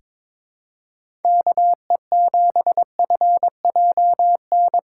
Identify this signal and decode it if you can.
Morse code, heard as a keyed tone, about 22 wpm, 700 Hz. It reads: KE7FJN
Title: USA Call Signs - 6 Characters - 3x Character Spacing - 22wpm